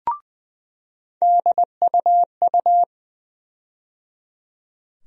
Morse code, heard as a keyed tone, about 20 wpm, 700 Hz. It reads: DUU
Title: USA Call Sign Postfixes - 3 Letters 20wpm